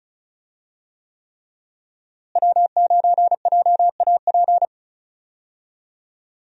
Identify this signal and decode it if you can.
Morse code, heard as a keyed tone, about 36 wpm, 700 Hz. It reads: W9JAP
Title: USA Call Signs - 5 Characters - Repeated 3x - 35wpm